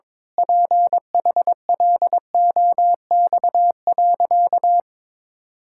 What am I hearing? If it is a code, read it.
Morse code, heard as a keyed tone, about 22 wpm, 700 Hz. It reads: PHLOX.
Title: Pangram Challenge v2 - 22wpm